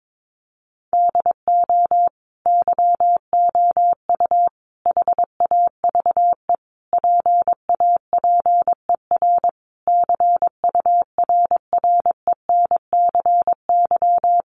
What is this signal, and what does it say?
Morse code, heard as a keyed tone, about 22 wpm, 700 Hz. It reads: DO YOU HAVE PAPER CURRENCY
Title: Sentences from Top 1500 Words 22wpm